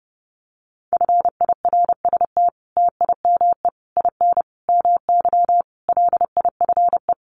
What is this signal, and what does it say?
Morse code, heard as a keyed tone, about 28 wpm, 700 Hz. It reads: FIRST TIME IN MY LIFE
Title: Sets of 5 Words Encore2 30wpm